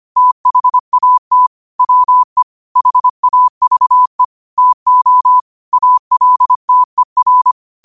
Morse code, heard as a keyed tone, about 25 wpm, 1000 Hz. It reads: THAT WE HAVE TO ALTER
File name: qotdfiles / 25wpm